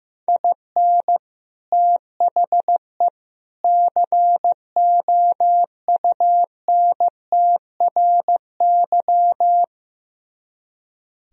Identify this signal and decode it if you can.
Morse code, heard as a keyed tone, about 15 wpm, 700 Hz. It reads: IN THE COUNTRY
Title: Speed-Racing - Sets of 3 Words 15wpm